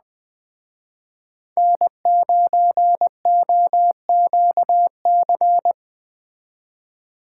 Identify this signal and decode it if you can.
Morse code, heard as a keyed tone, about 20 wpm, 700 Hz. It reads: N9OQC